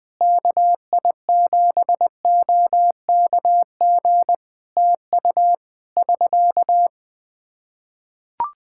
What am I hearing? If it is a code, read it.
Morse code, heard as a keyed tone, about 20 wpm, 700 Hz. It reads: KI7OKG TU <SK>
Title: K1USN SST QSO Practice 20wpm